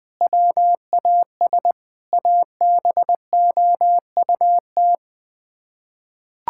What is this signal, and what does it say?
Morse code, heard as a keyed tone, about 20 wpm, 700 Hz. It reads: WAS ABOUT
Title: Sets of 8 Words - 20wpm